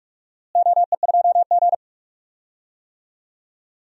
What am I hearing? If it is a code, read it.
Morse code, heard as a keyed tone, about 46 wpm, 700 Hz. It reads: OE2G
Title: DXCC-100 Call Signs - 4 Characters - Repeated 3x - 45wpm